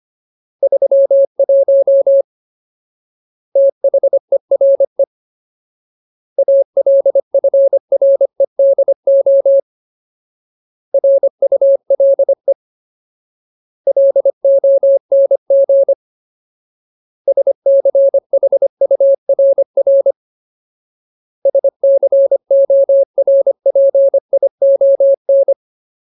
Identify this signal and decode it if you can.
Morse code, heard as a keyed tone, about 25 wpm, 550 Hz. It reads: 31 THERE ALFREDO RULE LONG SCHURR SCORPION
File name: Ultimate Word List / 25WPM Ultimate